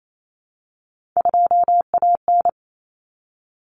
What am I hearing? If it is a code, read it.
Morse code, heard as a keyed tone, about 28 wpm, 700 Hz. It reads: 2AN
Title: Common Cut Numbers - 3-Digit 28wpm